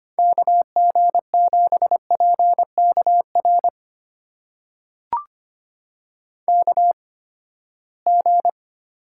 Morse code, heard as a keyed tone, about 25 wpm, spaced as for 17 wpm, 700 Hz. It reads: KG7PKR KG
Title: USA Call Signs - 6 Characters - 8x Character Spacing - 25wpm